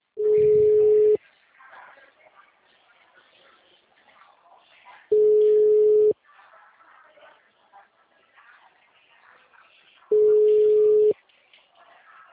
Ringback tone (Europe, 425 Hz).